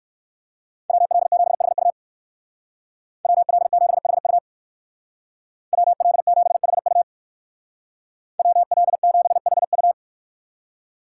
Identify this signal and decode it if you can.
Morse code, heard as a keyed tone, about 64 wpm, 700 Hz. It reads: WL7HU WL7HU WL7HU WL7HU